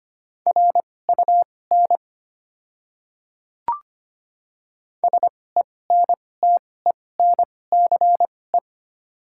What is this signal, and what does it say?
Morse code, heard as a keyed tone, about 25 wpm, spaced as for 18 wpm, 690 Hz. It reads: RUN SENTENCE
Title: 2x Character Spacing - Top 300 Words - 25wpm